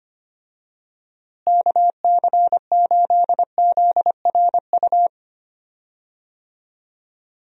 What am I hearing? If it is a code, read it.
Morse code, heard as a keyed tone, about 25 wpm, 700 Hz. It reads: KC8ZRU
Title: USA Call Signs - 6 Characters - Repeated 3x - 25wpm